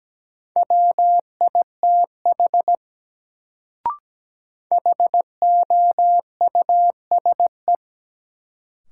Morse code, heard as a keyed tone, about 17 wpm, 700 Hz. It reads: WITH HOUSE
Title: Sets of 2 Words - New Word Drill - 17wpm